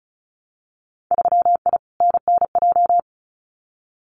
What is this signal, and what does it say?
Morse code, heard as a keyed tone, about 36 wpm, 700 Hz. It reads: 3I NNJ
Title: Winter Field Day Exchanges 35wpm